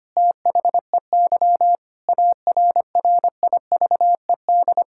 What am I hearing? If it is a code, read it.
Morse code, heard as a keyed tone, about 25 wpm, 700 Hz. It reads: THEY ARRIVED
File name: practice / 25wpm / Sentences with 4-Word Phrases - 25wpm